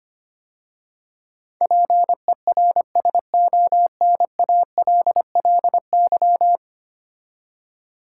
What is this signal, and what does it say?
Morse code, heard as a keyed tone, about 25 wpm, 700 Hz. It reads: PERSONALLY
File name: 10 Letter Words - First 3 Letters Given - 25wpm